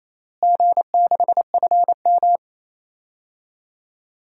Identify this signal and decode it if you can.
Morse code, heard as a keyed tone, about 27 wpm, 700 Hz. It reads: G6FM
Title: DXCC-100 Call Signs - 4 Characters - Repeated 3x - 28wpm